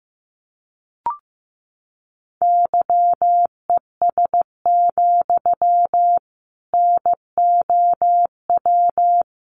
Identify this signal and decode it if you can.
Morse code, heard as a keyed tone, about 15 wpm, 700 Hz. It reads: YES, NOW
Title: Sentences from Top 800 Words 15wpm